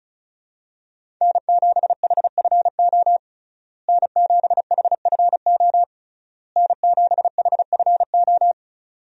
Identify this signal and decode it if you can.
Morse code, heard as a keyed tone, about 36 wpm, 700 Hz. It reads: N7HFO N7HFO N7HFO
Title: USA Call Signs - 5 Characters - Repeated 3x - 35wpm